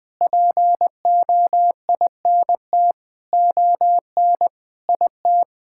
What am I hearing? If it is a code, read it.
Morse code, heard as a keyed tone, about 20 wpm, 700 Hz. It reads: POINT ON IT
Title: Sentences with 9-Word Phrases - 20wpm